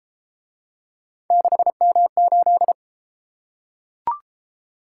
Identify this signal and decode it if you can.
Morse code, heard as a keyed tone, about 31 wpm, 700 Hz. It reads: 6M8